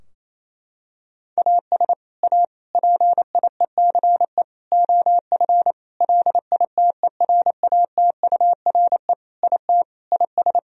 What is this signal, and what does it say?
Morse code, heard as a keyed tone, about 28 wpm, 700 Hz. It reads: AS A PIECE OF LITERATURE IT IS
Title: Sentences from Top 1000 Words - Review Part 3 - 28wpm